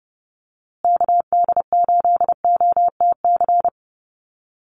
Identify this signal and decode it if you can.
Morse code, heard as a keyed tone, about 30 wpm, 700 Hz. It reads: KD8OTC
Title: USA Call Signs Repeated 3x 30wpm